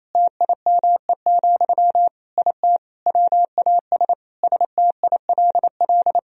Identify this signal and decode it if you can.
Morse code, heard as a keyed tone, about 27 wpm, 690 Hz. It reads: TIME, IT WAS STILL
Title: Sentences from Top 1000 Words - Review Part 3 - 28wpm